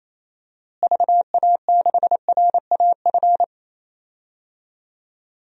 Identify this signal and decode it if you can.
Morse code, heard as a keyed tone, about 28 wpm, 700 Hz. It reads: VA6RAF
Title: DXCC-100 Call Signs - 6 Characters - Repeated 2x - 28wpm